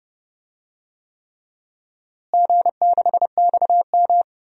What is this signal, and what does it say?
Morse code, heard as a keyed tone, about 30 wpm, 700 Hz. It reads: G6XM
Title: DXCC-100 Call Signs - 4 Characters - Repeated 3x - 30wpm